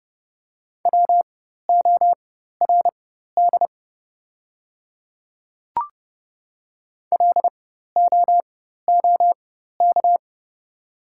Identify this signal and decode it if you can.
Morse code, heard as a keyed tone, about 30 wpm, spaced as for 14 wpm, 690 Hz. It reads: WORD LOOK
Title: 4x Character Spacing - Top 200 Words - 30wpm